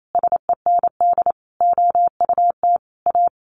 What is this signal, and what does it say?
Morse code, heard as a keyed tone, about 28 wpm, 700 Hz. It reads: SEND OUT A
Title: Sentences from Top 1500 Words 28wpm